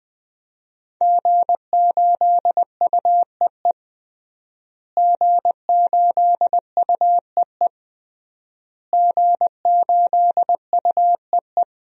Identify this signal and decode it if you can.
Morse code, heard as a keyed tone, about 20 wpm, 700 Hz. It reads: G8UEE G8UEE G8UEE
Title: DXCC-100 Call Signs - 5 Characters - Repeated 3x - 20wpm